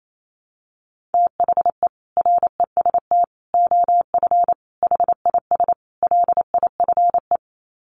Morse code, heard as a keyed tone, about 28 wpm, 700 Hz. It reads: THE REST OF HIS LIFE
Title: Sets of 5 Words 28wpm